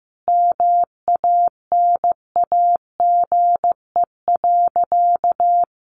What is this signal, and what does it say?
Morse code, heard as a keyed tone, about 15 wpm, 700 Hz. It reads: MANAGE.
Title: Sentences from Top 1400 Words 15wpm